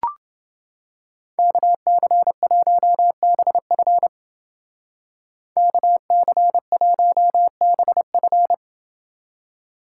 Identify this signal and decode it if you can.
Morse code, heard as a keyed tone, about 28 wpm, 700 Hz. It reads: KC1BF KC1BF